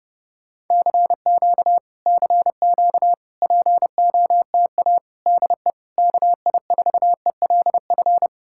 Morse code, heard as a keyed tone, about 29 wpm, 700 Hz. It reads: CQ CQ POTA DE KI4ELF